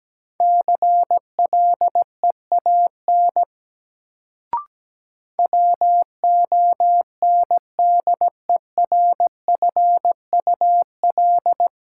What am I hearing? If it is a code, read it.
Morse code, heard as a keyed tone, about 17 wpm, 700 Hz. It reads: CLEAN WONDERFUL
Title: Sentences from Top 1100 Words - New Word Drill - 17wpm